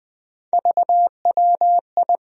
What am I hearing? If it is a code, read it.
Morse code, heard as a keyed tone, about 20 wpm, 700 Hz. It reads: VWI
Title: USA Call Sign Postfixes - 3 Letters 20wpm